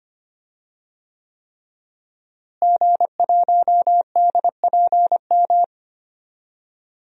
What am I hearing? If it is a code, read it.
Morse code, heard as a keyed tone, about 25 wpm, 700 Hz. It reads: G1DPM
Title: DXCC-100 Call Signs - 5 Characters - Repeated 3x - 25wpm